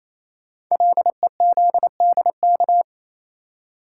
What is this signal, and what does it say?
Morse code, heard as a keyed tone, about 28 wpm, 700 Hz. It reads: LEZDK
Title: Code Groups - 5 Letters - 28wpm